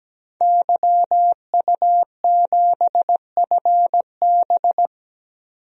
Morse code, heard as a keyed tone, about 17 wpm, 700 Hz. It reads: YU7FB
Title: DXCC-100 Call Signs - 5 Characters - 17wpm